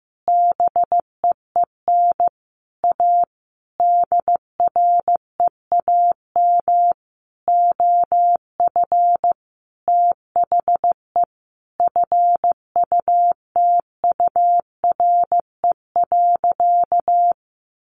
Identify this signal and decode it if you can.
Morse code, heard as a keyed tone, about 15 wpm, 700 Hz. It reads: BEEN A DREAM OF THE FUTURE.
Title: Sentences from Top 1500 Words 15wpm